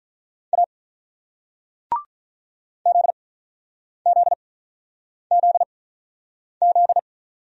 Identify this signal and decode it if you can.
Morse code, heard as a keyed tone, about 42 wpm, spaced as for 9 wpm, 700 Hz. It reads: A ZZZZ
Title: Speed-Racing Single Letters 35wpm